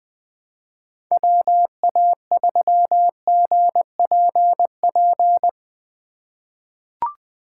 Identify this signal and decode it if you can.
Morse code, heard as a keyed tone, about 20 wpm, 700 Hz. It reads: WA3GPP
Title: K1USN SST QSO Practice - 20wpm